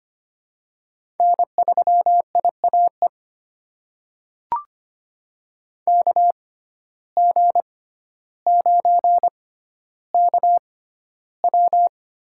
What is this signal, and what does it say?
Morse code, heard as a keyed tone, about 25 wpm, spaced as for 13 wpm, 700 Hz. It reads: N3IAE KG9KW
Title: USA Call Signs - 5 Characters - 6x Character Spacing - 25wpm